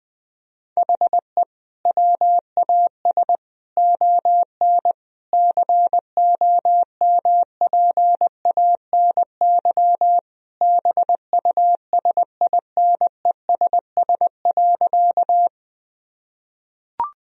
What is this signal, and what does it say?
Morse code, heard as a keyed tone, about 20 wpm, 700 Hz. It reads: HE WAS ON COMPANY BUSINESS.
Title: Sentences from Top 1000 Words - Review Part 3 - 20wpm